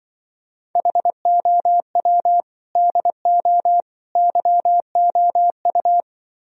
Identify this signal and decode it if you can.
Morse code, heard as a keyed tone, about 24 wpm, 700 Hz. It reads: HOW DO YOU
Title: Speed-Racing - Sets of 3 Words 20wpm